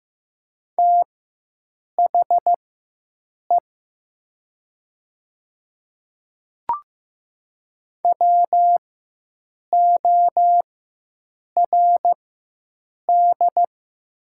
Morse code, heard as a keyed tone, about 15 wpm, spaced as for 7 wpm, 700 Hz. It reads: THE WORD
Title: 4x Character Spacing - Top 200 Words - 15wpm